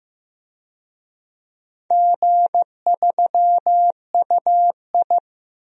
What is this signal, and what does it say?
Morse code, heard as a keyed tone, about 16 wpm, 700 Hz. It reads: G3UI
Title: DXCC-100 Call Signs - 4 Characters - Repeated 2x - 15wpm